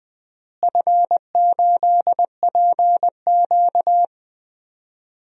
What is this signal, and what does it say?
Morse code, heard as a keyed tone, about 20 wpm, 700 Hz. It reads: F8PQ
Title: DXCC-100 Call Signs - 4 Characters - Repeated 3x - 20wpm